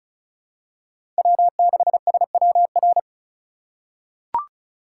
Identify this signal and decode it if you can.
Morse code, heard as a keyed tone, about 36 wpm, 700 Hz. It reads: W6SWR
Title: USA Call Signs - 4x Character Spacing - 35wpm